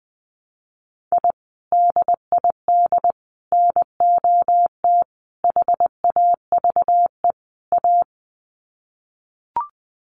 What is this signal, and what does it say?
Morse code, heard as a keyed tone, about 20 wpm, 700 Hz. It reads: I DID NOT HAVE A
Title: Sets of 5 Words 20wpm